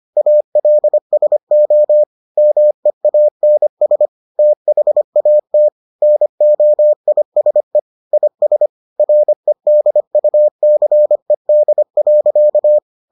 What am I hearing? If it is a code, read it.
Morse code, heard as a keyed tone, about 25 wpm, 600 Hz. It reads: ALSO MEANS THAT NOISE IS REDUCED.